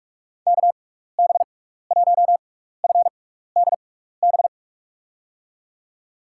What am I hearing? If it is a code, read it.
Morse code, heard as a keyed tone, about 41 wpm, spaced as for 15 wpm, 700 Hz. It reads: KB1FDB
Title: USA Call Signs - 6 Characters - 6x Character Spacing - 45wpm